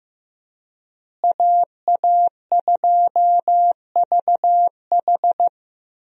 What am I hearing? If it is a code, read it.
Morse code, heard as a keyed tone, about 15 wpm, 700 Hz. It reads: AA2VH
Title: USA Call Signs - 5 Characters - Repeated 3x - 15wpm